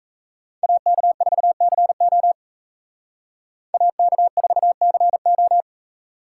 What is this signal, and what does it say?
Morse code, heard as a keyed tone, about 35 wpm, 700 Hz. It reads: AK4CO AK4CO
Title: Speed Racing - USA 2x2 Call Signs 28wpm